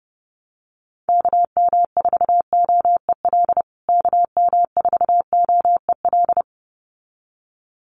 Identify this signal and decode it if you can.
Morse code, heard as a keyed tone, about 30 wpm, 700 Hz. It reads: KM4OEL KM4OEL
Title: USA Call Signs Repeated 2x 30wpm